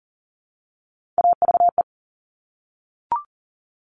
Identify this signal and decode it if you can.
Morse code, heard as a keyed tone, about 39 wpm, 700 Hz. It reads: AVE
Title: Top Three Letter Combinations in Words 40wpm